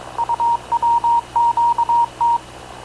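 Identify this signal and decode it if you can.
Morse code, heard as a keyed tone, about 22 wpm, 940 Hz. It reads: UWQT
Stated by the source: Havana Radio / CLA sending Spanish press (PX) in Morse, recorded by Radio Officer Paul Dumesnil onboard CNAV QUEST/CZDO.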